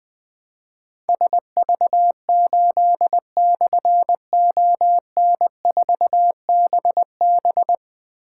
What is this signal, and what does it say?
Morse code, heard as a keyed tone, about 20 wpm, 700 Hz. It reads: SV8/ON4BB
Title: Difficult Call Signs 20wpm